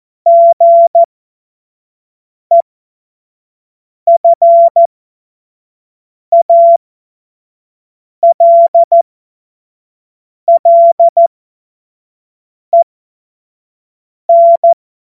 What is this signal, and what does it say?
Morse code, heard as a keyed tone, about 14 wpm, spaced as for 5 wpm, 680 Hz. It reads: GEFALLEN
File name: Morsekurs / Witze / Witz 24.Oktober 2023 25 F70